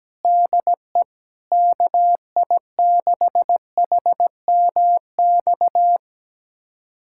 Morse code, heard as a keyed tone, about 17 wpm, 700 Hz. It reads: DE KI6HMX
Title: USA Callsign DE Callsign 17wpm